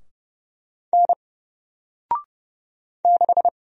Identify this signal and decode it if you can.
Morse code, heard as a keyed tone, about 30 wpm, 700 Hz. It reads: N 6
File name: Single-Letter-Number-30wpm